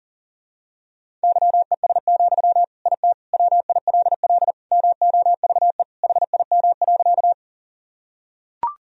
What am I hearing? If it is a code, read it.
Morse code, heard as a keyed tone, about 40 wpm, 700 Hz. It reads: YES, IT WILL MOVE HIM.
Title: Sentences from Top 1000 Words - Review Part 1 - 40wpm